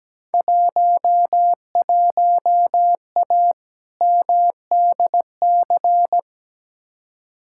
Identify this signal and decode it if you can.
Morse code, heard as a keyed tone, about 17 wpm, 700 Hz. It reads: 11A MDC
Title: Field Day Exchanges Encore - 17wpm